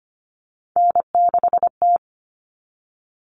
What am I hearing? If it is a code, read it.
Morse code, heard as a keyed tone, about 25 wpm, 700 Hz. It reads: N6T
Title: All Cut Numbers - 3-Digit 25wpm